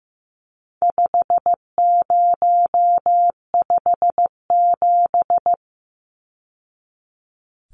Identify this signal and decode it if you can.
Morse code, heard as a keyed tone, about 15 wpm, 700 Hz. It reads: 5057
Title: Numbers - 4-Digit 15wpm